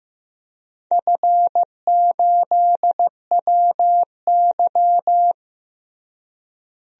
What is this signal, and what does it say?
Morse code, heard as a keyed tone, about 15 wpm, 700 Hz. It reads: F8WY
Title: DXCC-100 Call Signs - 4 Characters - Repeated 3x - 15wpm